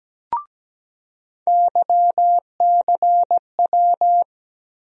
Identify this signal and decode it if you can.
Morse code, heard as a keyed tone, about 17 wpm, 700 Hz. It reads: YCW
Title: USA Call Sign Postfixes - 3 Letters 17wpm